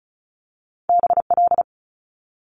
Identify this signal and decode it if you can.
Morse code, heard as a keyed tone, about 36 wpm, 700 Hz. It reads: BL
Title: Multi-letter Phonemes Quick Fire 35wpm